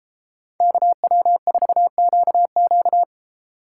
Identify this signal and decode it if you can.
Morse code, heard as a keyed tone, about 34 wpm, 700 Hz. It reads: KW4QQ
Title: Speed Racing - USA Call Signs 28wpm